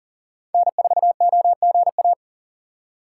Morse code, heard as a keyed tone, about 40 wpm, 700 Hz. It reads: N4OGA
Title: USA Call Signs - 5 Characters - 3x Character Spacing - 40wpm